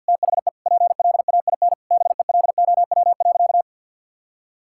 Morse code, heard as a keyed tone, about 51 wpm, 700 Hz. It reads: THE PLAIN BELOW.
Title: Sentences from Top 1000 Words - Review Part 5 - 50wpm